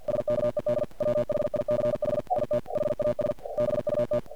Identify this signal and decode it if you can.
Morse code, heard as a keyed tone, about 45 wpm, 590 Hz, with noise. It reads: SKED WHICH IT HAS BUT
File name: archived / CRSnet